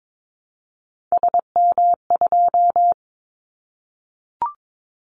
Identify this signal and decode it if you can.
Morse code, heard as a keyed tone, about 22 wpm, 700 Hz. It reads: SM2
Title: Callsign Prefixes for Top 25 DXCC - Repeated 3x - 22wpm